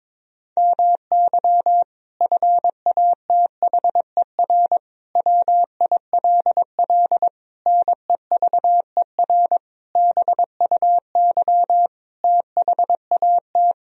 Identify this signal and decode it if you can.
Morse code, heard as a keyed tone, about 22 wpm, 700 Hz. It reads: MY FATHER WILL NEVER BUY THAT
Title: Sentences from Top 1000 Words - Review Part 4 - 22wpm